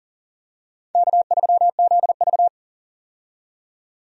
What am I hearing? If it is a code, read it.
Morse code, heard as a keyed tone, about 40 wpm, 700 Hz. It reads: K3ZV
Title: USA Call Signs - 4 Characters - 40wpm